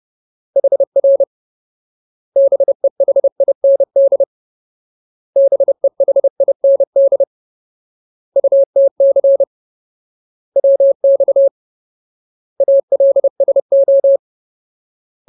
Morse code, heard as a keyed tone, about 30 wpm, 550 Hz. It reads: HR BEHIND BEHIND UTC WX ALSO
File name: Ultimate Word List / 30WPM Ultimate